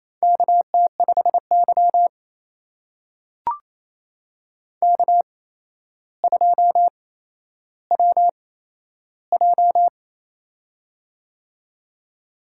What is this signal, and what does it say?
Morse code, heard as a keyed tone, about 28 wpm, spaced as for 12 wpm, 700 Hz. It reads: KT5Y K2WJ
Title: USA Call Signs - 4 Characters - 8x Character Spacing - 28wpm